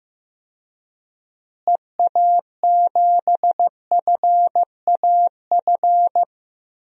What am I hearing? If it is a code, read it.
Morse code, heard as a keyed tone, about 15 wpm, 700 Hz. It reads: EA7FAF
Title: DXCC-100 Call Signs - 6 Characters - Repeated 3x - 15wpm